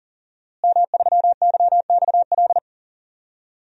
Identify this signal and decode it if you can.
Morse code, heard as a keyed tone, about 38 wpm, 700 Hz. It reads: M3YXL
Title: DXCC-100 Call Signs - 5 Characters - Repeated 2x - 40wpm